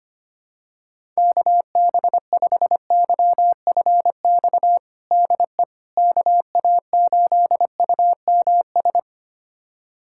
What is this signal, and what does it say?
Morse code, heard as a keyed tone, about 25 wpm, 700 Hz. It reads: KB5YFX DE KA8UMS
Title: USA Callsign DE Callsign - 6 Characters - 25wpm